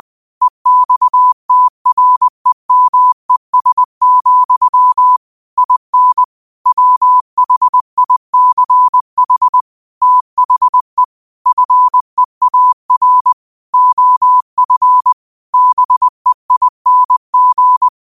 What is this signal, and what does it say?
Morse code, heard as a keyed tone, about 20 wpm, 1000 Hz. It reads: EXTREMES, IN WHICH THE FEAR OF BEING
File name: qotdfiles / 20wpm